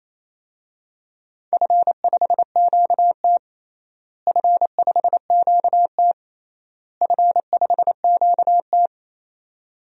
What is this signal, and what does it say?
Morse code, heard as a keyed tone, about 28 wpm, 700 Hz. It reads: F5QT F5QT F5QT
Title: DXCC-100 Call Signs - 4 Characters - Repeated 3x - 28wpm